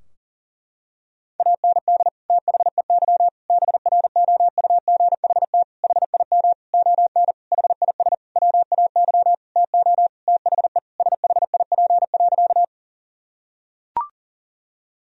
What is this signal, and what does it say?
Morse code, heard as a keyed tone, about 40 wpm, 700 Hz. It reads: AND THEY BROUGHT HIM ON HIS WAY TO THE SHIP.
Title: Sentences from Top 1000 Words - Review Part 2 - 40wpm